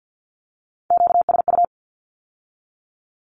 Morse code, heard as a keyed tone, about 50 wpm, 700 Hz. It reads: QSU
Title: Q-Codes with Spoken Meaning - 50wpm